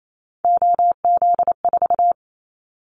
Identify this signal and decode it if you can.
Morse code, heard as a keyed tone, about 28 wpm, 700 Hz. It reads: OZ4
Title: Callsign Prefixes for Top 50 DXCC - 28wpm